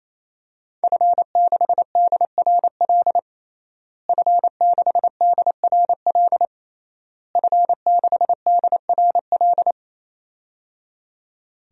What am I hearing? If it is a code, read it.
Morse code, heard as a keyed tone, about 28 wpm, 700 Hz. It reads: F6DRL F6DRL F6DRL
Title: DXCC-100 Call Signs - 5 Characters - Repeated 3x - 28wpm